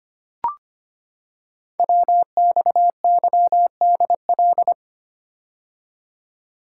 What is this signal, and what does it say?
Morse code, heard as a keyed tone, about 25 wpm, 700 Hz. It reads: WXYDL
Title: Code Groups - 5 Letters - 25wpm